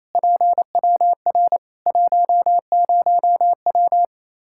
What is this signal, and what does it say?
Morse code, heard as a keyed tone, about 28 wpm, 700 Hz. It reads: PWR 10W
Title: Practice QSO - Exchange 2 - Info - 28wpm